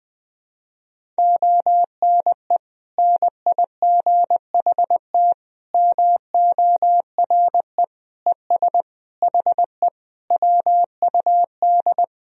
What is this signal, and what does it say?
Morse code, heard as a keyed tone, about 20 wpm, 700 Hz. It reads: ONE NIGHT MORE ES HE WUD
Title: Sentences from Top 300 Words and Abbreviations 20wpm